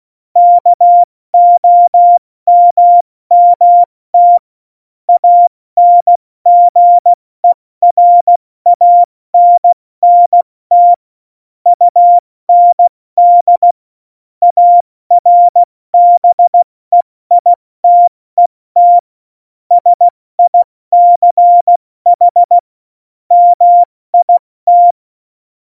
Morse code, heard as a keyed tone, about 16 wpm, 700 Hz. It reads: KOMMT ANGERANNT UND ARBEITET SICH MIT
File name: Morsekurs / Witze / Witz 13. November 2018, 70 F80